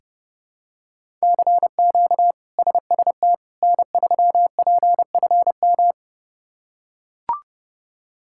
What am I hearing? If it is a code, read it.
Morse code, heard as a keyed tone, about 30 wpm, 700 Hz. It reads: CQ SST N3PFM
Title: K1USN SST QSO Practice - 30wpm